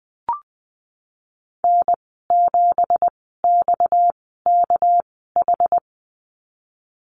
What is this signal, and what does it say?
Morse code, heard as a keyed tone, about 19 wpm, spaced as for 14 wpm, 700 Hz. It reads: N7XKH
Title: 231 Lesson 37g - Letter x - Call Signs 20wpm